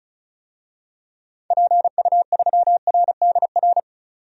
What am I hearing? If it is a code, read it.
Morse code, heard as a keyed tone, about 36 wpm, 690 Hz. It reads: PU3RDR
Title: DXCC-100 Call Signs - 6 Characters - 35wpm